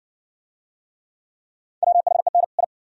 Morse code, heard as a keyed tone, about 59 wpm, 700 Hz. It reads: WHEN I